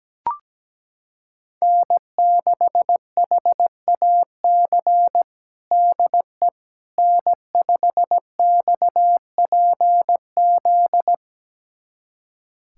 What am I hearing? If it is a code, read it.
Morse code, heard as a keyed tone, about 17 wpm, 700 Hz. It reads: N6HAC DE N5XPZ
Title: USA Callsign DE Callsign - 5 Characters - 17wpm